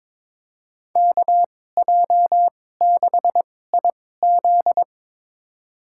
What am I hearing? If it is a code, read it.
Morse code, heard as a keyed tone, about 22 wpm, spaced as for 16 wpm, 700 Hz. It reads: KJ6IZ
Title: USA Call Signs - 2x Character Spacing - 22wpm